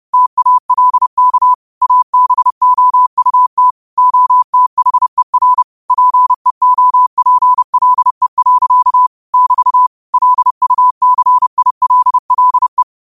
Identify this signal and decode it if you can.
Morse code, heard as a keyed tone, about 30 wpm, 1000 Hz. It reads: TALK ABOUT OTHER PEOPLE. <BT> LUCILLE
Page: Quotes for Wed, 13 Aug 2025 in Morse Code at 30 words per minute.